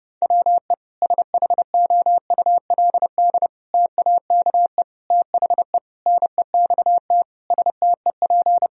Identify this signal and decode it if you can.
Morse code, heard as a keyed tone, about 32 wpm, 700 Hz. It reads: WE SHOULD TAKE THE NEXT STEP
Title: Sentences from Top 1000 Words - Review Part 2 - 30wpm